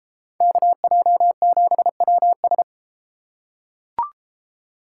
Morse code, heard as a keyed tone, about 32 wpm, 690 Hz. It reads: KJ7WS